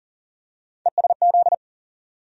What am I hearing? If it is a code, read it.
Morse code, heard as a keyed tone, about 40 wpm, 700 Hz. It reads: ESZ
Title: USA Call Sign Postfixes - 3 Letters 40wpm